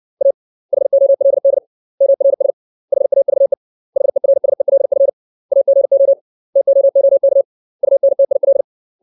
Morse code, heard as a keyed tone, about 60 wpm, 550 Hz. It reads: A HOLD MRS HAVE HERSELF AGO TOOK UNTIL
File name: 632 words / 632 Words 60WPM